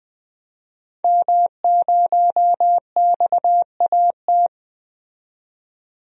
Morse code, heard as a keyed tone, about 20 wpm, 700 Hz. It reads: M0XAT
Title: DXCC-100 Call Signs - 5 Characters - 20wpm